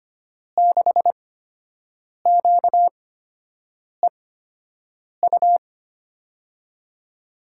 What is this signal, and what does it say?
Morse code, heard as a keyed tone, about 25 wpm, spaced as for 7 wpm, 700 Hz. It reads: 6QEU
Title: USA Call Signs - 6 Characters - 8x Character Spacing - 25wpm